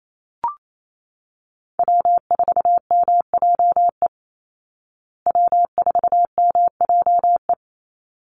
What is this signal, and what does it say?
Morse code, heard as a keyed tone, about 28 wpm, 700 Hz. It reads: W4MJE W4MJE
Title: Call Signs Repeated 3x - 28wpm